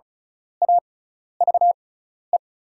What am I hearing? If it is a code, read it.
Morse code, heard as a keyed tone, about 34 wpm, spaced as for 12 wpm, 700 Hz. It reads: AVE
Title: 6x Character Spacing - Top 500 Words - 35wpm